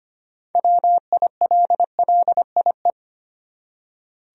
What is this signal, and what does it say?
Morse code, heard as a keyed tone, about 25 wpm, 700 Hz. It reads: WILLIE
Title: USA Common Names - 25wpm